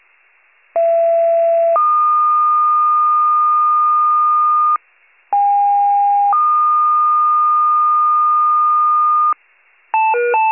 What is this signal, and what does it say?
Two-tone page, heard as a one-second tone, then a three-second tone, about 670 then 1220 Hz, about 810 then 1220 Hz. It is heard from 0.8 to 9.3 s.
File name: pager / 39 RESCUE / wcalert